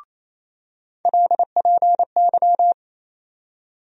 Morse code, heard as a keyed tone, about 28 wpm, 700 Hz. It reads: LPY
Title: USA Call Sign Postfixes - 3 Letters 28wpm